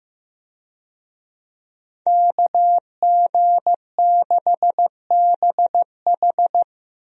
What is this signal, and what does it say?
Morse code, heard as a keyed tone, about 15 wpm, 700 Hz. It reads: KG6BH
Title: USA Call Signs - 5 Characters - Repeated 2x - 15wpm